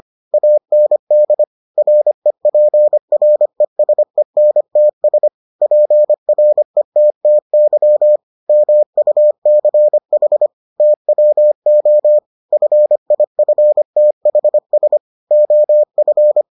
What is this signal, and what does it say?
Morse code, heard as a keyed tone, about 25 wpm, 600 Hz. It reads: AND REPRESENTS PRETTY MUCH TWO FIFTHS OF